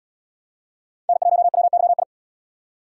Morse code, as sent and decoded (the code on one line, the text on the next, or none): -. .---- .-- -.-- ..
N1WYI